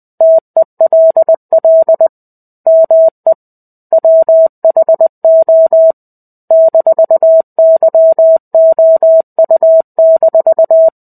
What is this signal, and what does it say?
Morse code, heard as a keyed tone, about 20 wpm, 650 Hz. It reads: TELL ME WHO -YOU-
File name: 05-Chapter-05 20 WPM